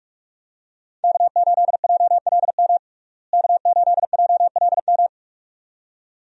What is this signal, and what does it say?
Morse code, heard as a keyed tone, about 46 wpm, 700 Hz. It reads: K8JLM K8JLM
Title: USA Call Signs - 5 Characters - Repeated 2x - 45wpm